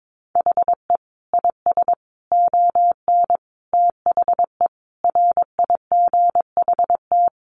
Morse code, heard as a keyed tone, about 22 wpm, 700 Hz. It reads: HE IS ON THE RIGHT
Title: Sentences from Top 1100 Words 22wpm